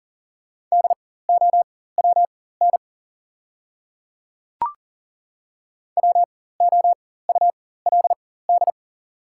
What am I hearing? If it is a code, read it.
Morse code, heard as a keyed tone, about 40 wpm, spaced as for 19 wpm, 690 Hz. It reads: DOWN WOULD
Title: 4x Character Spacing - Top 200 Words - 40wpm